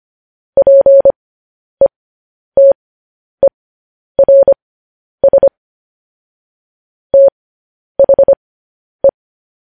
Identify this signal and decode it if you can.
Morse code, heard as a keyed tone, about 25 wpm, spaced as for 10 wpm, 550 Hz. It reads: PETERS THE
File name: podcast / News headlines-1x-25x10